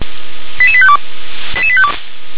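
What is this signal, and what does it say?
Five-tone selective call (ZVEI), heard as quick runs of five tones, tones of about 70 ms. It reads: 88852
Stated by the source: Five-tone ZVEI selective call received by SDR.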